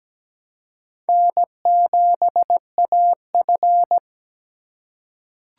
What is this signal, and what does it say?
Morse code, heard as a keyed tone, about 17 wpm, 700 Hz. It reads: N7AF
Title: USA Call Signs - 4 Characters - 4x Character Spacing - 17wpm